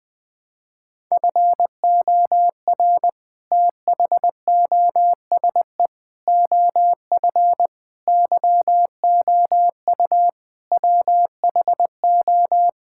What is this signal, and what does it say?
Morse code, heard as a keyed tone, about 20 wpm, 700 Hz. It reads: FOR THOSE OF YOU WHO
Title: Sentences with 6-Word Phrases - 20wpm